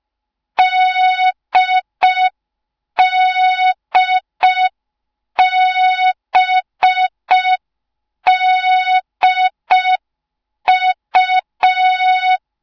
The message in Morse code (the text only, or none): DDBDU